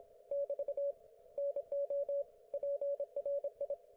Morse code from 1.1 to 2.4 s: NO